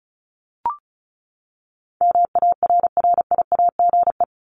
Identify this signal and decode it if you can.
Morse code, heard as a keyed tone, about 34 wpm, 700 Hz. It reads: MARRIAGE
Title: Top 751-1000 Words 35wpm